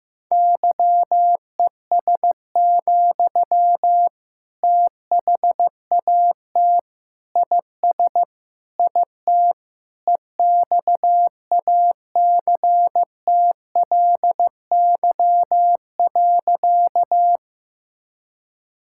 Morse code, heard as a keyed tone, about 15 wpm, 700 Hz. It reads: YES, THAT IS IT EXACTLY.
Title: Sentences from Top 1000 Words - Review Part 6 - 15wpm